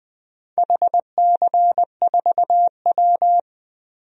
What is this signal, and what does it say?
Morse code, heard as a keyed tone, about 21 wpm, 700 Hz. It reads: HC4W
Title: DXCC-100 Call Signs - 4 Characters - Repeated 3x - 20wpm